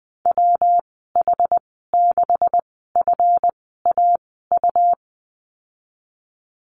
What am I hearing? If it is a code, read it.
Morse code, heard as a keyed tone, about 20 wpm, spaced as for 14 wpm, 700 Hz. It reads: WH6FAU
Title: 147 Lesson 24g - Letter y - Call Signs 20wpm